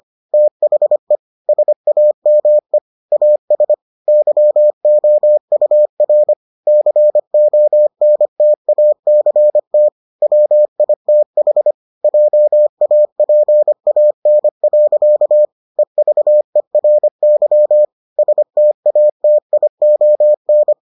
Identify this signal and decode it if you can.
Morse code, heard as a keyed tone, about 25 wpm, 600 Hz. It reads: THE SAME AS YOUR CONTACT WITH JAPAN. EVERY STATION